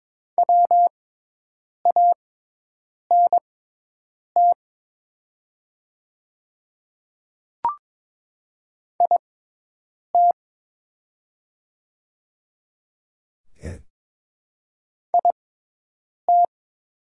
Morse code, heard as a keyed tone, about 22 wpm, spaced as for 8 wpm, 700 Hz. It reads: WANT IT IT
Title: 6x Character Spacing - Top 200 Words - 22wpm